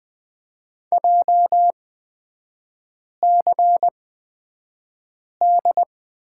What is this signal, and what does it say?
Morse code, heard as a keyed tone, about 20 wpm, spaced as for 5 wpm, 700 Hz. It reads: JCD
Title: 236 Lesson 38e - Letter q - Three Characters 20wpm